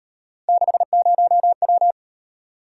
Morse code, heard as a keyed tone, about 36 wpm, 700 Hz. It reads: 60W